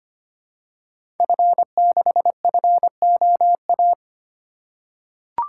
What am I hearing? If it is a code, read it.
Morse code, heard as a keyed tone, about 25 wpm, 700 Hz. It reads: F6FOA
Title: Code Groups - 5 Letter:Numbers - 25wpm